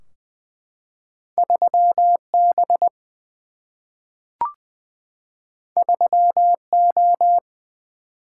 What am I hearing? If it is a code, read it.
Morse code, heard as a keyed tone, about 20 wpm, 700 Hz. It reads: 3 B 3 O
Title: Single-Letter Single-Number 20wpm